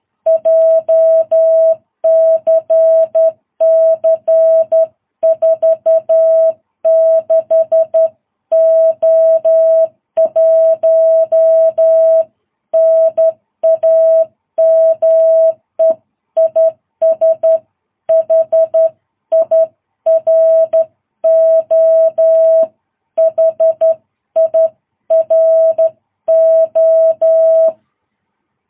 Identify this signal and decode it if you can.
Morse code, heard as a keyed tone, about 12 wpm, 660 Hz. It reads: JCC 46O1 NAME IS HIRO HIRO
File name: jh8cbh / 2022cw